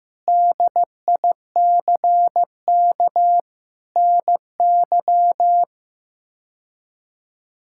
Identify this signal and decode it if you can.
Morse code, heard as a keyed tone, about 15 wpm, 700 Hz. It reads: DICK NY
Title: K1USN SST QSO Practice - 15wpm